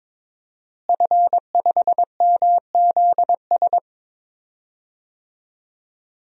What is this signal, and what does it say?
Morse code, heard as a keyed tone, about 22 wpm, 700 Hz. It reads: F5MZS
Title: DXCC-100 Call Signs - 5 Characters - Repeated 3x - 22wpm